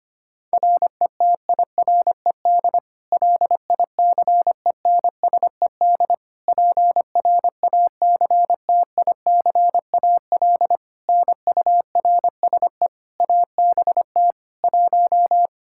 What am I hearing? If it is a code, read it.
Morse code, heard as a keyed tone, about 25 wpm, 700 Hz. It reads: RETIRED LICENSED PRACTICAL NURSE ABT 1
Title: Practice QSO - Exchange 3 - More Info - 25wpm